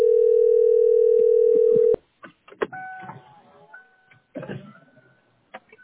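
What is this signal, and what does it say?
Ringback tone (North America), 440 + 480 Hz.
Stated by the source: Call recording from the tesspb PBX.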